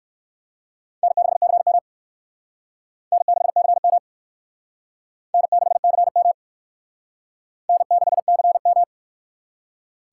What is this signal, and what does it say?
Morse code, heard as a keyed tone, about 56 wpm, 700 Hz. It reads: N6CM N6CM N6CM N6CM